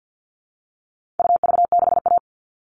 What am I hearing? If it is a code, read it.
Morse code, heard as a keyed tone, about 50 wpm, 700 Hz. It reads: UV6A
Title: All Cut Numbers - 4-Digit 50wpm